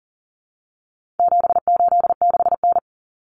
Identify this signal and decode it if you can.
Morse code, heard as a keyed tone, about 40 wpm, 700 Hz. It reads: 786N
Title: Common Cut Numbers - 4-Digit 40wpm